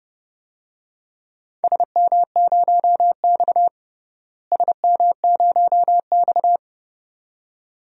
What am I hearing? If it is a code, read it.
Morse code, heard as a keyed tone, about 30 wpm, 700 Hz. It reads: SM0X SM0X
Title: DXCC-100 Call Signs - 4 Characters - Repeated 2x - 30wpm